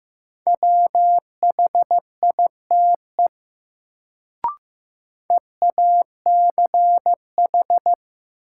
Morse code, heard as a keyed tone, about 15 wpm, 700 Hz. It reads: WHITE EACH
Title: Sets of 2 Words - New Word Drill - 15wpm